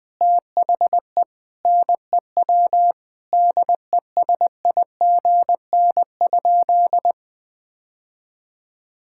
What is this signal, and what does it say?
Morse code, heard as a keyed tone, about 20 wpm, 700 Hz. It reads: THE NEW DESIGN?
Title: Sentences with 5-Word Phrases - 20wpm